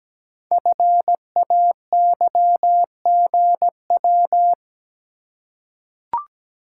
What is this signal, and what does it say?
Morse code, heard as a keyed tone, about 17 wpm, 700 Hz. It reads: FAYGW
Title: Code Groups - 5 Letters - 17wpm